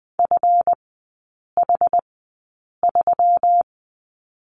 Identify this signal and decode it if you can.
Morse code, heard as a keyed tone, about 20 wpm, spaced as for 8 wpm, 700 Hz. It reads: FH3
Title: 159 Lesson 26e - Letter g - Three Characters 20wpm